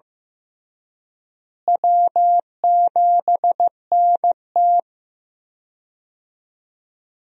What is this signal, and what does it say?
Morse code, heard as a keyed tone, about 15 wpm, 700 Hz. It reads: W7NT
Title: USA Call Signs - 4 Characters - Repeated 3x - 15wpm